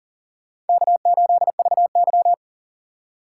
Morse code, heard as a keyed tone, about 40 wpm, 700 Hz. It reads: K8VY
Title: USA Call Signs - 3x Character Spacing - 40wpm